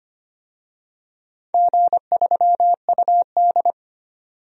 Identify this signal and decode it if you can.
Morse code, heard as a keyed tone, about 25 wpm, 700 Hz. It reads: G3UD
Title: DXCC-100 Call Signs - 4 Characters - 25wpm